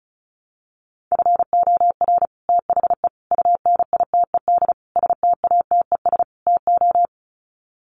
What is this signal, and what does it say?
Morse code, heard as a keyed tone, about 34 wpm, 700 Hz. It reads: FOR THE UNITED STATES TO
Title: Sets of 5 Words Encore 35wpm